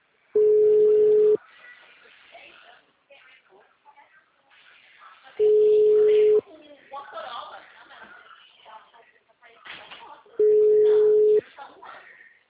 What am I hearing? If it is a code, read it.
Ringback tone (Europe, 425 Hz), with speech.
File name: out- -2063- - - .64702